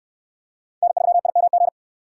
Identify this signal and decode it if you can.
Morse code, heard as a keyed tone, about 68 wpm, 700 Hz. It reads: N3IMG